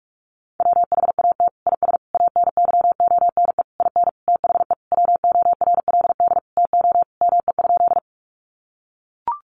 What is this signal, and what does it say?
Morse code, heard as a keyed tone, about 44 wpm, 700 Hz. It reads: WHAT IS ANYONE IN THE WORLD TO ME?
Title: Sentences from Top 900 Words 45wpm